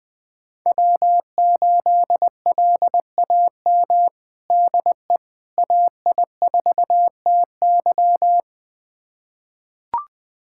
Morse code, heard as a keyed tone, about 21 wpm, 700 Hz. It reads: W8LAM DE AI4TY
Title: USA Callsign DE Callsign - 5 Characters - 20wpm